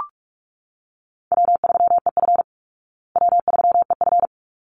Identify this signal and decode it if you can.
Morse code, heard as a keyed tone, about 44 wpm, 700 Hz. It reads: W3EF W3EF
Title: Call Signs Repeated 3x - 45wpm